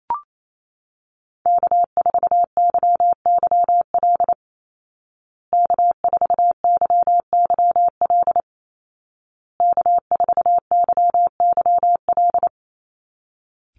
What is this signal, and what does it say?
Morse code, heard as a keyed tone, about 28 wpm, 700 Hz. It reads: K4YYL K4YYL K4YYL
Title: Call Signs Repeated 2x - 28wpm